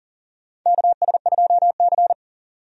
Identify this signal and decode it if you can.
Morse code, heard as a keyed tone, about 40 wpm, 700 Hz. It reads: KS2C
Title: USA Call Signs - 4 Characters - Repeated 2x - 40wpm